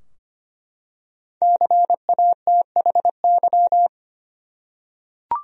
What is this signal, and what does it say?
Morse code, heard as a keyed tone, about 25 wpm, 700 Hz. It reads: CATHY
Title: USA Common Names - 25wpm